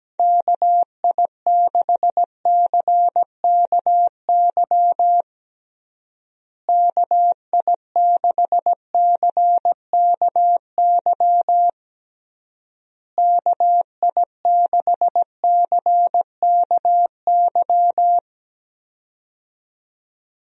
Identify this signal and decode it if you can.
Morse code, heard as a keyed tone, about 17 wpm, 700 Hz. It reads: KI6CKY KI6CKY KI6CKY
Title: USA Call Signs - 6 Characters - Repeated 3x - 17wpm